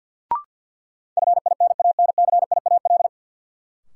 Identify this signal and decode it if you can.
Morse code, heard as a keyed tone, about 52 wpm, 700 Hz. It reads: FINANCIAL
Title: Sentences from Top 1000 Words - Review Part 5 - New Word Drill - 50wpm